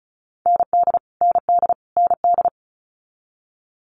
Morse code, heard as a keyed tone, about 36 wpm, 700 Hz. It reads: ND ND ND
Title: Multi-letter Phonemes Repeated 3x 35wpm